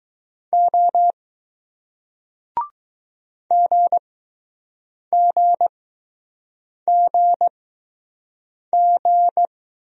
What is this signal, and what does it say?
Morse code, heard as a keyed tone, about 20 wpm, spaced as for 6 wpm, 700 Hz. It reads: O GGGG